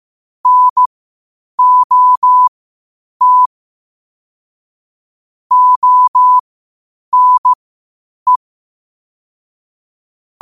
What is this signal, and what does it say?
Morse code, heard as a keyed tone, about 15 wpm, spaced as for 8 wpm, 1000 Hz. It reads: NOT ONE
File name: qotdfiles / 8wpm